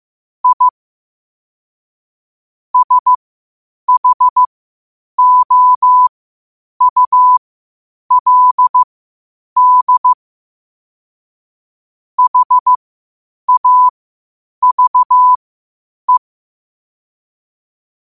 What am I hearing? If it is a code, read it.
Morse code, heard as a keyed tone, about 15 wpm, spaced as for 8 wpm, 1000 Hz. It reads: I SHOULD HAVE
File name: qotdfiles / 8wpm